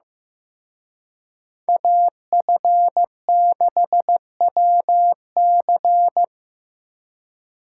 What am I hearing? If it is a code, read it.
Morse code, heard as a keyed tone, about 18 wpm, 700 Hz. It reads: AF6WC
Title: Speed Racing - USA 2x2 Call Signs 15wpm